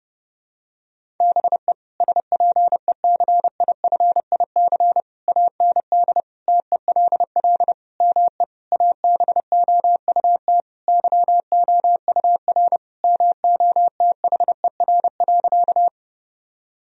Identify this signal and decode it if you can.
Morse code, heard as a keyed tone, about 32 wpm, 700 Hz. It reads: BE SPECIFIC AND TELL ME ABOUT YOUR MOTHER.
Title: Sentences from Top 1000 Words - Review Part 3 - 30wpm